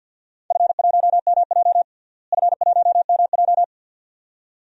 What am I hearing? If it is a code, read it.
Morse code, heard as a keyed tone, about 47 wpm, 700 Hz. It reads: F1MJ F1MJ
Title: DXCC-100 Call Signs - 4 Characters - Repeated 2x - 50wpm